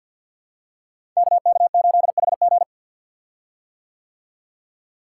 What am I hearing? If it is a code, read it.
Morse code, heard as a keyed tone, about 52 wpm, 700 Hz. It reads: KK8HG
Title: Code Groups - 5 Letter:Numbers - 50wpm